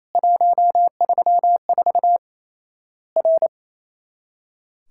Morse code, heard as a keyed tone, about 27 wpm, 700 Hz. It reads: 134 R
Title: Worked All Europe DX Contest - QTC Exchange - 28wpm